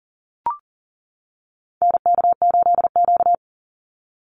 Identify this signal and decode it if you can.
Morse code, heard as a keyed tone, about 40 wpm, 700 Hz. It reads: NK8Q
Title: NA SOTA Activator Callsigns 40wpm